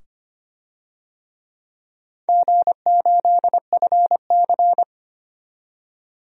Morse code, heard as a keyed tone, about 25 wpm, 700 Hz. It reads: G8FC
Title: DXCC-100 Call Signs - 4 Characters - 25wpm